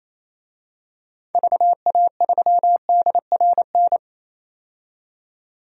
Morse code, heard as a keyed tone, about 28 wpm, 700 Hz. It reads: VA3DRN
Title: DXCC-100 Call Signs - 6 Characters - 28wpm